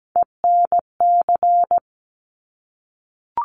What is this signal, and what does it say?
Morse code, heard as a keyed tone, about 17 wpm, 700 Hz. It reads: ENC
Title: Top Three Letter Combinations in Words 17wpm